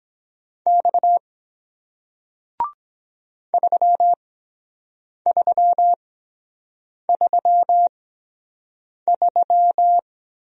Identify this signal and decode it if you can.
Morse code, heard as a keyed tone, about 23 wpm, spaced as for 7 wpm, 700 Hz. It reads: X 3333